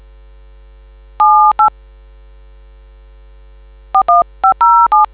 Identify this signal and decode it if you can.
DTMF, heard as telephone touch tones, tones of about 140 ms. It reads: 78 41507